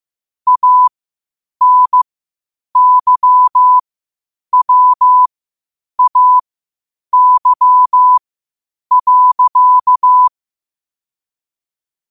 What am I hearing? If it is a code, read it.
Morse code, heard as a keyed tone, about 15 wpm, spaced as for 8 wpm, 1000 Hz. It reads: ANYWAY.
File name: qotdfiles / 8wpm